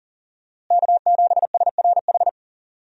Morse code, heard as a keyed tone, about 40 wpm, 700 Hz. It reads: K7SRH
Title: USA Call Signs - 5 Characters - Repeated 3x - 40wpm